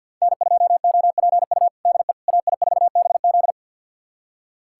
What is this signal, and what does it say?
Morse code, heard as a keyed tone, about 52 wpm, 700 Hz. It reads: N2OPU DE AI4BZ
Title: USA Callsign DE Callsign 50wpm